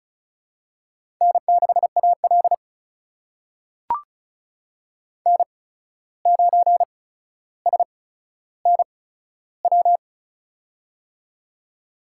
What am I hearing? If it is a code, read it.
Morse code, heard as a keyed tone, about 36 wpm, spaced as for 14 wpm, 700 Hz. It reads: N6AL N9SNW
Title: USA Call Signs - 8x Character Spacing - 35wpm